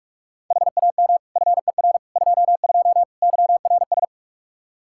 Morse code, heard as a keyed tone, about 46 wpm, 700 Hz. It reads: HAM FER 22 YRS
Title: Practice QSO - Exchange 3 - More Info - 45wpm